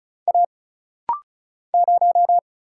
Morse code, heard as a keyed tone, about 36 wpm, 700 Hz. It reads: A 0
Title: Single Letter-Number - Rapid-Fire 35wpm